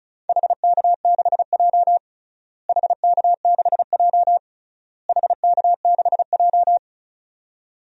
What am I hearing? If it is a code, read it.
Morse code, heard as a keyed tone, about 34 wpm, 700 Hz. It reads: HK6J HK6J HK6J
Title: DXCC-100 Call Signs - 4 Characters - Repeated 3x - 35wpm